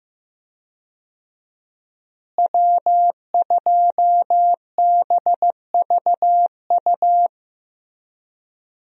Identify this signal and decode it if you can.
Morse code, heard as a keyed tone, about 15 wpm, 700 Hz. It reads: W2BVU
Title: USA Call Signs - 5 Characters - Repeated 2x - 15wpm